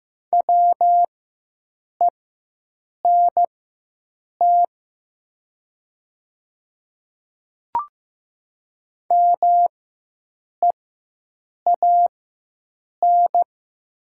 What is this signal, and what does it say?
Morse code, heard as a keyed tone, about 15 wpm, spaced as for 7 wpm, 700 Hz. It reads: WENT MEAN
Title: 4x Character Spacing - Top 200 Words - 15wpm